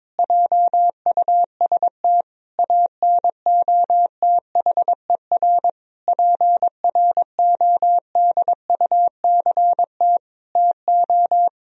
Morse code, heard as a keyed tone, about 22 wpm, 700 Hz. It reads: JUST ANOTHER PRODUCT TO
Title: Sentences from Top 1000 Words - Review Part 3 - 22wpm